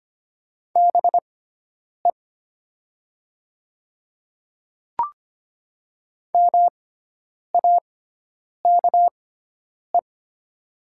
Morse code, heard as a keyed tone, about 25 wpm, spaced as for 9 wpm, 700 Hz. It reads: BE MAKE
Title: 6x Character Spacing - Top 300 Words - 25wpm